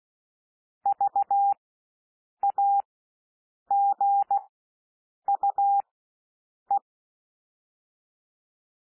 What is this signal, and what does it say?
Morse code, heard as a keyed tone, about 16 wpm, spaced as for 7 wpm, 800 Hz. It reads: VAGUE
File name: cwmp3 / W1 SOUND WORDS 16 wpm